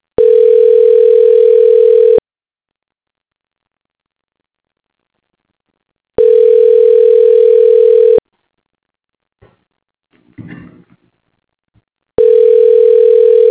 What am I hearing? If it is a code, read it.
Ringback tone (North America), 440 + 480 Hz.